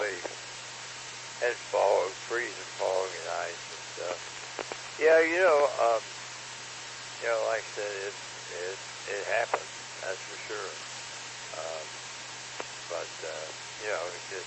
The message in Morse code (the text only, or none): none